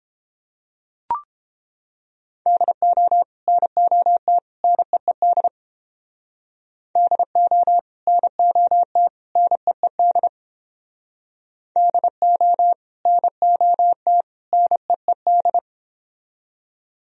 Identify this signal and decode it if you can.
Morse code, heard as a keyed tone, about 29 wpm, 700 Hz. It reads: DO NOT NEED DO NOT NEED DO NOT NEED